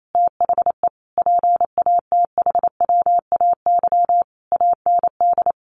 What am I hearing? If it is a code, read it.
Morse code, heard as a keyed tone, about 28 wpm, 700 Hz. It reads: THE PATHWAY AND
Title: A Sound Of Thunder 28wpm